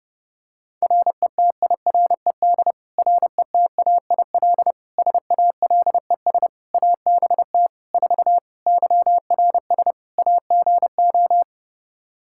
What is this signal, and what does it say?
Morse code, heard as a keyed tone, about 32 wpm, 700 Hz. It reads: RETIRED RETAIL SALES ABT 4 YRS AGO
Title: Practice QSO - Exchange 3 - More Info - 30wpm